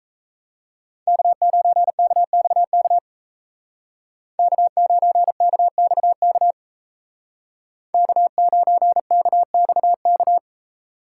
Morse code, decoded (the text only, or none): K9KXK K9KXK K9KXK